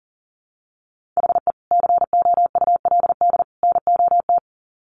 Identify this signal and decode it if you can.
Morse code, heard as a keyed tone, about 40 wpm, 700 Hz. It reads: HE COULD NOT
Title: Sets of 3 Words 40wpm